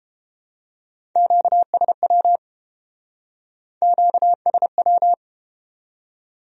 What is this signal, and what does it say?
Morse code, heard as a keyed tone, about 31 wpm, 700 Hz. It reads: QSW QSW